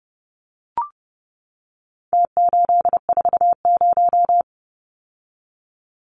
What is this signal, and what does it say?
Morse code, heard as a keyed tone, about 30 wpm, 700 Hz. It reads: T840
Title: Numbers - Contest Style - 4-Digit 30wpm